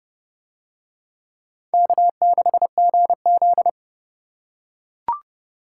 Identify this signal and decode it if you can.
Morse code, heard as a keyed tone, about 27 wpm, 700 Hz. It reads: K6GZ